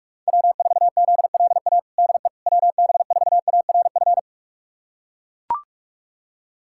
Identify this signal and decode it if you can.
Morse code, heard as a keyed tone, about 43 wpm, 700 Hz. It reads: W4ZLA DE WB4ARF
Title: USA Callsign DE Callsign 45wpm